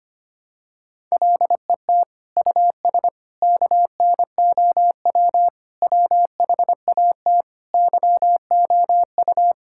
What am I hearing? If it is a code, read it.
Morse code, heard as a keyed tone, about 25 wpm, 700 Hz. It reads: LET US KNOW WHAT YOU
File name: practice / 25wpm / Sentences with 9-Word Phrases - 25wpm